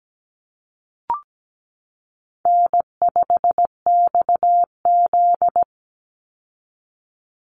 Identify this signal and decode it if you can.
Morse code, heard as a keyed tone, about 17 wpm, 700 Hz. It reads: N5XZ
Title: Easy Call Signs Repeated 2x - 17wpm